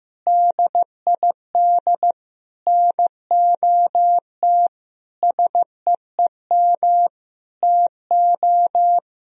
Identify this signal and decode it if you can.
Morse code, heard as a keyed tone, about 15 wpm, 700 Hz. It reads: DID NOT SEEM TO
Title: Sentences from Top 1000 Words - Review Part 1 - 15wpm